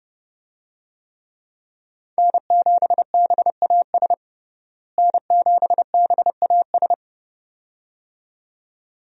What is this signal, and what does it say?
Morse code, heard as a keyed tone, about 30 wpm, 700 Hz. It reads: N7BAS N7BAS
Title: USA Call Signs - 5 Characters - Repeated 2x - 30wpm